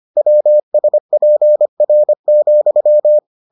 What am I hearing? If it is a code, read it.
Morse code, heard as a keyed tone, about 25 wpm, 600 Hz. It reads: WSPR,